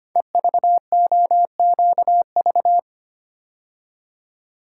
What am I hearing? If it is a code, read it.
Morse code, heard as a keyed tone, about 25 wpm, 700 Hz. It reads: EVOQV
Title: Code Groups - 5 Letters - 25wpm